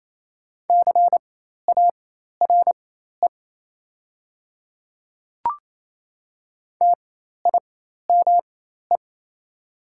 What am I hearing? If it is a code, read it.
Morse code, heard as a keyed tone, about 28 wpm, spaced as for 13 wpm, 700 Hz. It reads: CARE TIME
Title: 4x Character Spacing - Top 500 Words - 28wpm